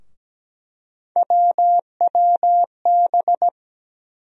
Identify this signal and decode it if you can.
Morse code, heard as a keyed tone, about 17 wpm, 700 Hz. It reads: WWB
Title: USA Call Sign Postfixes - 3 Letters 17wpm